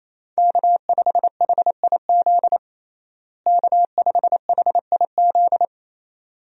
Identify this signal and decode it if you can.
Morse code, heard as a keyed tone, about 28 wpm, 700 Hz. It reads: K5HIZ K5HIZ
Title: USA Call Signs - 5 Characters - Repeated 2x - 28wpm